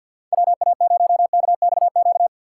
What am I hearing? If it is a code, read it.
Morse code, heard as a keyed tone, about 51 wpm, 700 Hz. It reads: WA0KXQ
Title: USA Call Signs - 6 Characters - 4x Character Spacing - 50wpm